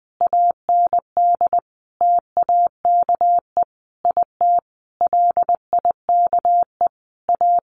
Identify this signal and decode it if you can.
Morse code, heard as a keyed tone, about 20 wpm, 700 Hz. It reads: AND TAKE IT LIKE A
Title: Sentences from Top 300 Words 20wpm